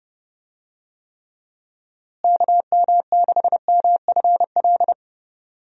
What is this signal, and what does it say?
Morse code, heard as a keyed tone, about 31 wpm, 700 Hz. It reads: KM6MFL
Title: USA Call Signs - 6 Characters - Repeated 2x - 30wpm